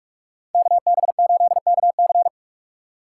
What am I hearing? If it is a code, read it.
Morse code, heard as a keyed tone, about 46 wpm, 700 Hz. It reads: KB8KC
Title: Speed Racing - USA 2x2 Call Signs 45wpm